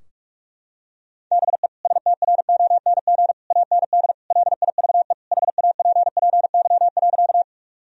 Morse code, heard as a keyed tone, about 47 wpm, 700 Hz. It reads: BE STRONG AND LIVE HAPPY.
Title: Sentences from Top 1000 Words - Review Part 3 - 45wpm